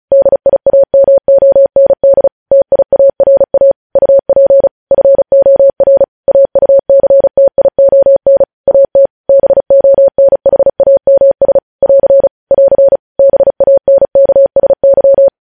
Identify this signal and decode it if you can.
Morse code, heard as a keyed tone, about 36 wpm, 550 Hz. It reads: DIAMOND TIARA UP FOR AUCTION AT BONHAMS <AR> <AR> BANKSY
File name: podcast / News headlines-1x-35x35